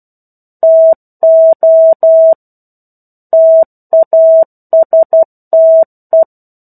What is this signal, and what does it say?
Morse code, heard as a keyed tone, about 12 wpm, 650 Hz. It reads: TO TASTE
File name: 01-Chapter-01 12 WPM